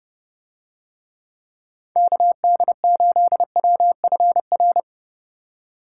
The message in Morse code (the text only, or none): KD8WFR